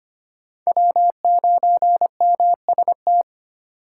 Morse code, heard as a keyed tone, about 34 wpm, spaced as for 30 wpm, 700 Hz. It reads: W9MST
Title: Speed Racing - USA 1x3 Call Signs 25wpm